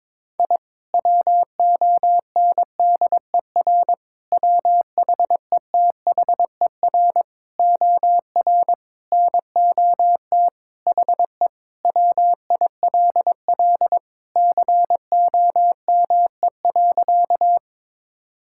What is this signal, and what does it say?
Morse code, heard as a keyed tone, about 22 wpm, 700 Hz. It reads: I WONDER WHETHER OR NOT HE WILL COME.
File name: practice / 22wpm / Sentences with 3-Word Phrases - 22wpm